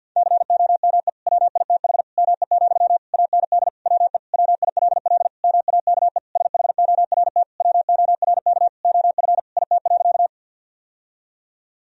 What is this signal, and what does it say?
Morse code, heard as a keyed tone, about 46 wpm, 700 Hz. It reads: COME WITH ME, AND WE WILL MAKE SHORT WORK OF IT.
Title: Sentences from Top 1000 Words - Review Part 2 - 50wpm